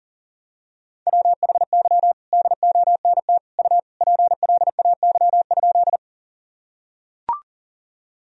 Morse code, heard as a keyed tone, about 37 wpm, 700 Hz. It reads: WHY DONT U PLAY?
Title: Sentences from Top 300 Words and Abbreviations 40wpm